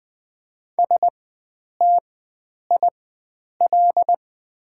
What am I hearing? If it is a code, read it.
Morse code, heard as a keyed tone, about 20 wpm, spaced as for 9 wpm, 700 Hz. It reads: STIL
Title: 4x Character Spacing - Top 300 Words - 20wpm